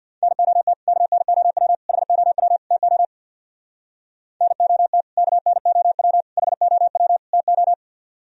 Morse code, heard as a keyed tone, about 64 wpm, 700 Hz. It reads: NOT KNOW HOW TO NOT KNOW HOW TO